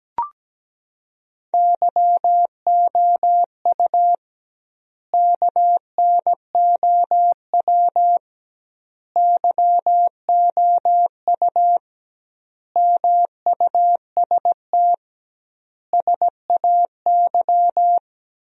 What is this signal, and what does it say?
Morse code, heard as a keyed tone, about 17 wpm, 700 Hz. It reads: YOU KNOW YOU MUST SAY
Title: 2x Word-Spacing - Sentences from Top 500 Words - 17wpm